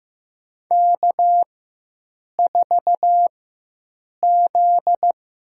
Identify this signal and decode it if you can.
Morse code, heard as a keyed tone, about 15 wpm, spaced as for 7 wpm, 700 Hz. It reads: K4Z
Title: USA Call Signs - 4 Characters - 4x Character Spacing - 15wpm